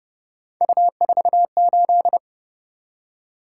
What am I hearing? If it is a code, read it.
Morse code, heard as a keyed tone, about 31 wpm, 700 Hz. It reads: U48
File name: Speed-Racing - Three Letter-Number Combinations - 25wpm